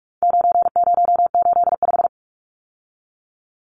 Morse code, heard as a keyed tone, about 44 wpm, 700 Hz. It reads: 9085
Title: Numbers - 4-Digit 45wpm